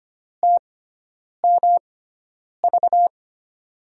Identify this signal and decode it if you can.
Morse code, heard as a keyed tone, about 25 wpm, spaced as for 9 wpm, 700 Hz. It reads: TMV
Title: USA Call Signs - 5 Characters - 6x Character Spacing - 25wpm